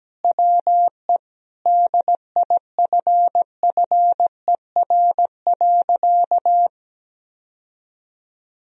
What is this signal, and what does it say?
Morse code, heard as a keyed tone, about 17 wpm, 700 Hz. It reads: WE DIFFER.
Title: Sentences from Top 1000 Words - Review Part 1 - 17wpm